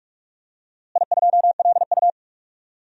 Morse code, heard as a keyed tone, about 42 wpm, 700 Hz. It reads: I2LU
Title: DXCC-100 Call Signs - 4 Characters - Repeated 2x - 45wpm